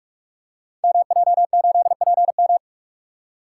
Morse code, heard as a keyed tone, about 44 wpm, 700 Hz. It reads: MJ8PM
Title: Code Groups - 5 Letter:Numbers - 45wpm